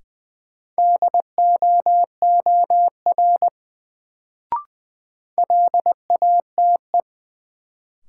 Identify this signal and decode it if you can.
Morse code, heard as a keyed tone, about 20 wpm, 700 Hz. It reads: DOOR LATE
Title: Sentences from Top 300 Words - New Word Drill - 20wpm